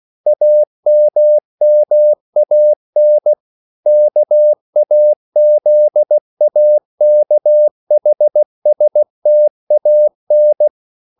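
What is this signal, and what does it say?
Morse code, heard as a keyed tone, about 16 wpm, 600 Hz. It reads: AMMAN KAZAKHSTAN
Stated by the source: CW practice of national capitals at 16 words per minute, from ON6ZQ.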